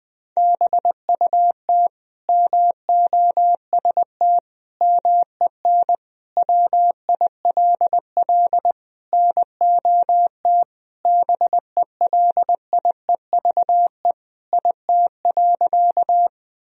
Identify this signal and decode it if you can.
Morse code, heard as a keyed tone, about 20 wpm, 700 Hz. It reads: BUT MOST MEN WILL NOT BELIEVE IT.
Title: Sentences from Top 1000 Words - Review Part 1 - 20wpm